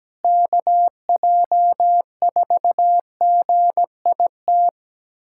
Morse code, heard as a keyed tone, about 17 wpm, 700 Hz. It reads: KJ4GIT
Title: USA Callsign DE Callsign - 6 Characters - 17wpm